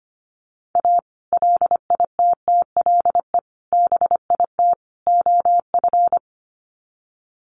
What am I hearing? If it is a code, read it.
Morse code, heard as a keyed tone, about 25 wpm, 700 Hz. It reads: A LITTLE BIT OF
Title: Sets of 4 Words 25wpm